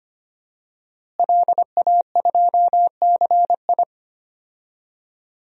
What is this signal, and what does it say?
Morse code, heard as a keyed tone, about 25 wpm, 700 Hz. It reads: LA2CI
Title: DXCC-100 Call Signs - 5 Characters - Repeated 2x - 25wpm